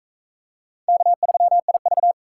Morse code, heard as a keyed tone, about 41 wpm, 700 Hz. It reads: K3IV